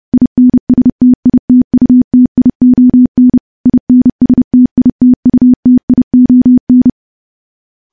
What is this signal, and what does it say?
Morse code, heard as a keyed tone, about 30 wpm, 250 Hz. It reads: INSTITUTION INSTITUTION
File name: morse / morse-training-30WPM